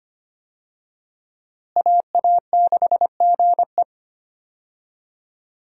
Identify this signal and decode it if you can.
Morse code, heard as a keyed tone, about 25 wpm, 700 Hz. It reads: AA6GE
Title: USA Call Signs - 5 Characters - Repeated 3x - 25wpm